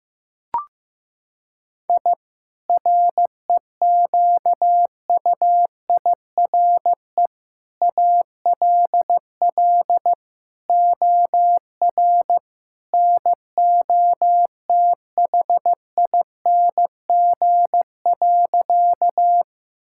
Morse code, heard as a keyed tone, about 15 wpm, 700 Hz. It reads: I REQUIRE ALL OR NOTHING.
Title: Sentences from Top 1000 Words - Review Part 6 - 15wpm